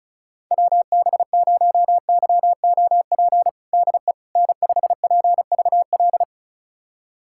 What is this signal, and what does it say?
Morse code, heard as a keyed tone, about 34 wpm, 700 Hz. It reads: WB0YOP DE N5PVL
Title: USA Callsign DE Callsign 35wpm